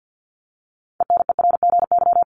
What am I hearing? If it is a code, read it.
Morse code, heard as a keyed tone, about 50 wpm, 700 Hz. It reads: ENERGY
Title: Top 751-1000 Words 50wpm